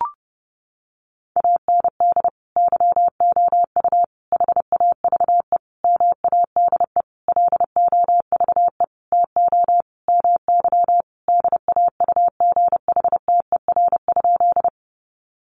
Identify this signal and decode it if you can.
Morse code, heard as a keyed tone, about 30 wpm, 700 Hz. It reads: AND YOU HAVE MADE LOVE TO MY DAUGHTER?
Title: Sentences from Top 1500 Words 30wpm